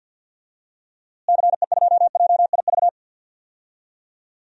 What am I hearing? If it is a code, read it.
Morse code, heard as a keyed tone, about 52 wpm, 700 Hz. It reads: CE2JIV
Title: DXCC-100 Call Signs - 6 Characters - Repeated 3x - 50wpm